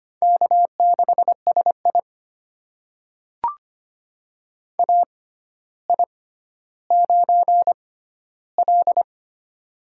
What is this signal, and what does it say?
Morse code, heard as a keyed tone, about 25 wpm, spaced as for 13 wpm, 700 Hz. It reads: K6SI AI9L
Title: USA Call Signs - 4 Characters - 6x Character Spacing - 25wpm